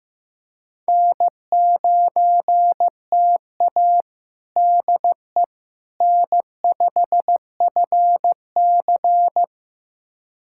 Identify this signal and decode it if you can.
Morse code, heard as a keyed tone, about 16 wpm, 700 Hz. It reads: N9TA DE N5FC
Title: USA Callsign DE Callsign - 4 Characters - 15wpm